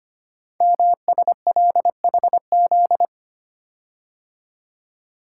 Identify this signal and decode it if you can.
Morse code, heard as a keyed tone, about 25 wpm, 700 Hz. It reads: MSLHZ
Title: Code Groups - 5 Letter:Numbers - 25wpm